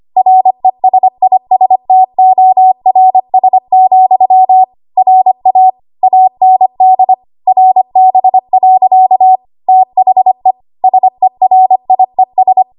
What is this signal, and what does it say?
Morse code, heard as a keyed tone, about 25 wpm, 750 Hz. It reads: RESISTORS, RA AND RB. THE SERIES
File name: Archive / 25 WPM / 25WPM